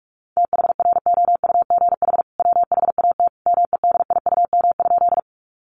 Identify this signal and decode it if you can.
Morse code, heard as a keyed tone, about 44 wpm, 700 Hz. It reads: THROUGH WHAT MEDIUM?
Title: Sentences from Top 900 Words 45wpm